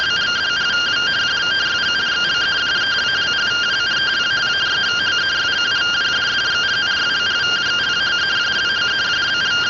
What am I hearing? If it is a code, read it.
Two-tone teleprinter signal, about 170 Hz shift.